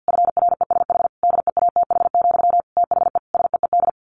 Morse code, heard as a keyed tone, about 50 wpm, 700 Hz. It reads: FRESH DEATH, THE SEED
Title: A Sound Of Thunder 50wpm